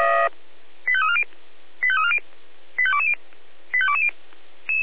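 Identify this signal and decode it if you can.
Five-tone selective call (ZVEI), heard as quick runs of five tones, tones of about 70 ms. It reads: 85340 85340 85220 85220
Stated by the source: ZVEI five-tone selective call received on SDR.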